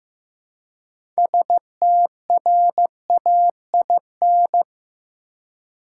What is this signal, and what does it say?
Morse code, heard as a keyed tone, about 15 wpm, 700 Hz. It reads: STRAIN
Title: Six Letter Words - 15wpm